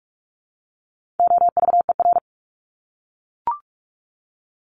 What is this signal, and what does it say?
Morse code, heard as a keyed tone, about 44 wpm, 700 Hz. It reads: OVER
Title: Top 300 Words 45wpm